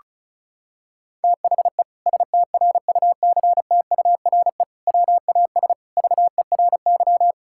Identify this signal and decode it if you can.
Morse code, heard as a keyed tone, about 34 wpm, 700 Hz. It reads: THE STRUCTURE WAS VERY
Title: Sentences from Top 1000 Words - Review Part 6 - 35wpm